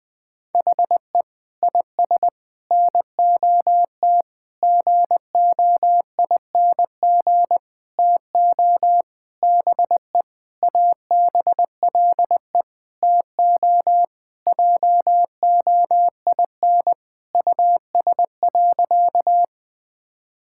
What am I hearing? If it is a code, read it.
Morse code, heard as a keyed tone, about 20 wpm, 700 Hz. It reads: HE IS NOT GOING TO BE ABLE TO JOIN US.
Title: Sentences with 5-Word Phrases - 20wpm